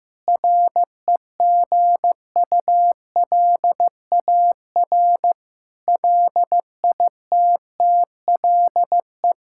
Morse code, heard as a keyed tone, about 15 wpm, 700 Hz. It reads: REGULAR LITTLE
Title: Sentences from Top 1000 Words - Review Part 6 - 15wpm